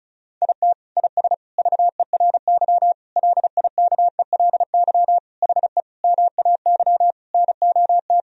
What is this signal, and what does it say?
Morse code, heard as a keyed tone, about 34 wpm, 700 Hz. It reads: IT IS VERY LIKELY HE MAY NOT
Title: Sentences from Top 1000 Words - Review Part 4 - 35wpm